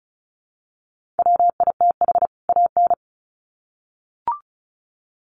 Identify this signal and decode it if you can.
Morse code, heard as a keyed tone, about 34 wpm, 700 Hz. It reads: WITH AN
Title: Sets of 2 Words 35wpm